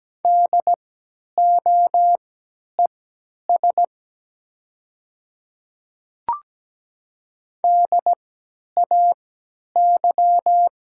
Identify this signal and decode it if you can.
Morse code, heard as a keyed tone, about 17 wpm, spaced as for 10 wpm, 700 Hz. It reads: DOES DAY
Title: 3x Character Spacing - Top 300 Words - 17wpm